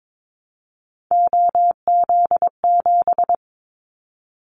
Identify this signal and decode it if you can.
Morse code, heard as a keyed tone, about 22 wpm, 700 Hz. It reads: OZ7
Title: Callsign Prefixes for Top 50 DXCC - 22wpm